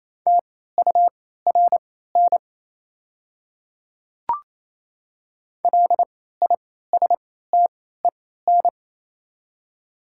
Morse code, heard as a keyed tone, about 28 wpm, spaced as for 16 wpm, 700 Hz. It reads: TURN LISTEN
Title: 3x Character Spacing - Top 500 Words - 28wpm